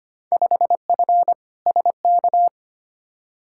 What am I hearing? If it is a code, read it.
Morse code, heard as a keyed tone, about 25 wpm, 700 Hz. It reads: 5F SK
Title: Field Day Exchanges Encore - 25wpm